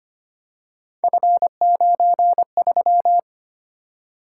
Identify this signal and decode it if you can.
Morse code, heard as a keyed tone, about 25 wpm, 700 Hz. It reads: F93
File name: Three Letter-Number Combinations - 25wpm